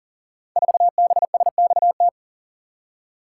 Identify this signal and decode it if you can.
Morse code, heard as a keyed tone, about 38 wpm, 700 Hz. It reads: 4BSXT
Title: Code Groups - 5 Letter:Numbers - 40wpm